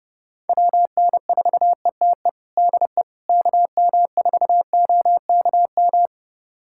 Morse code, heard as a keyed tone, about 30 wpm, 700 Hz. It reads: WN4ETE DE KM4OKM
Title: USA Callsign DE Callsign 30wpm